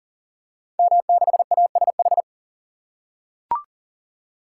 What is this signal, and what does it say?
Morse code, heard as a keyed tone, about 40 wpm, 700 Hz. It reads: M6ASH
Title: Code Groups - 5 Letter:Numbers - 40wpm